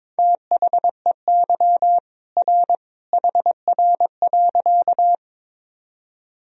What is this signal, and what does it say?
Morse code, heard as a keyed tone, about 22 wpm, 700 Hz. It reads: THEY R HR.
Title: Sentences from Top 200 Words and Abbreviations 22wpm